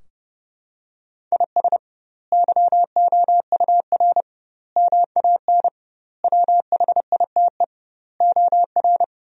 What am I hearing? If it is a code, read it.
Morse code, heard as a keyed tone, about 32 wpm, 700 Hz. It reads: IS YOUR MAN WHITE OR
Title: 2x Word-Spacing - Sentences from Top 400 Words - 30wpm